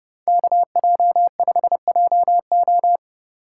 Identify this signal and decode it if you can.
Morse code, heard as a keyed tone, about 29 wpm, 700 Hz. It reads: KJ5JO